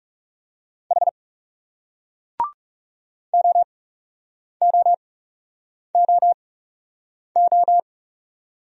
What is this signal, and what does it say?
Morse code, heard as a keyed tone, about 39 wpm, spaced as for 8 wpm, 700 Hz. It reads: H OOOO